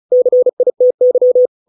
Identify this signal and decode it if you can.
Morse code, heard as a keyed tone, about 36 wpm, 500 Hz. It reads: CITY